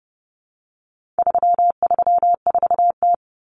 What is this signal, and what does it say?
Morse code, heard as a keyed tone, about 30 wpm, 700 Hz. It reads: 334T
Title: Common Cut Numbers - 4-Digit 30wpm